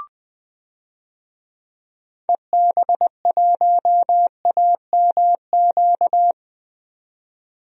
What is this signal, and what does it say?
Morse code, heard as a keyed tone, about 20 wpm, 700 Hz. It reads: EB1AMQ
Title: DXCC-100 Call Signs - 6 Characters - 20wpm